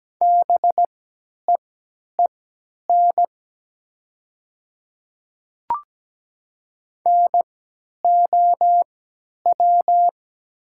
Morse code, heard as a keyed tone, about 17 wpm, spaced as for 10 wpm, 700 Hz. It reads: BEEN NOW
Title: 3x Character Spacing - Top 100 Words - 17wpm